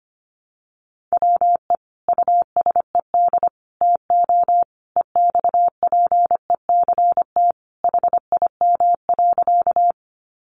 Morse code, heard as a keyed tone, about 25 wpm, 700 Hz. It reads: WE USED TO EXPECT HIM.
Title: Sentences from Top 1100 Words 25wpm